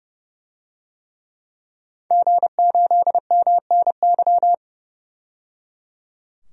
Morse code, heard as a keyed tone, about 30 wpm, 700 Hz. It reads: G8MNY
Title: DXCC-100 Call Signs - 5 Characters - 30wpm